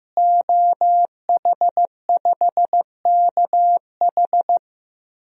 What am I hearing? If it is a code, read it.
Morse code, heard as a keyed tone, about 15 wpm, 700 Hz. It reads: OH5KH
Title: DXCC-100 Call Signs - 5 Characters - Repeated 2x - 15wpm